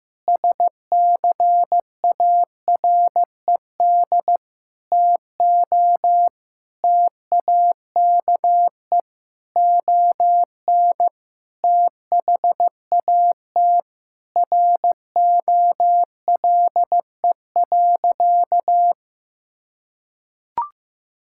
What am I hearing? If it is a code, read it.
Morse code, heard as a keyed tone, about 15 wpm, 700 Hz. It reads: SCARED TO TAKE ON THAT ROLE.
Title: Sentences from Top 1000 Words - Review Part 6 - 15wpm